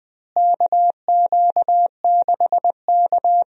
Morse code, heard as a keyed tone, about 21 wpm, 700 Hz. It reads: KQ6K
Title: USA Call Signs - 4 Characters - Repeated 2x - 20wpm